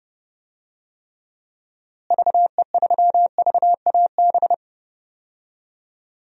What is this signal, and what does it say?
Morse code, heard as a keyed tone, about 32 wpm, 700 Hz. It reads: VE3VAB
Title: DXCC-100 Call Signs - 6 Characters - Repeated 3x - 30wpm